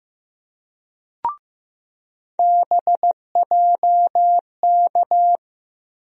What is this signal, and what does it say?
Morse code, heard as a keyed tone, about 15 wpm, 700 Hz. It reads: BJK
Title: USA Call Sign Postfixes - 3 Letters 15wpm